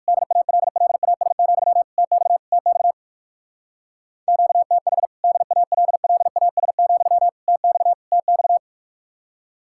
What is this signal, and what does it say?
Morse code, heard as a keyed tone, about 54 wpm, 700 Hz. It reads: DALLAS, TX TX QTH DALLAS, TX TX
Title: Speed-Racing Common QSO Phrases - 45wpm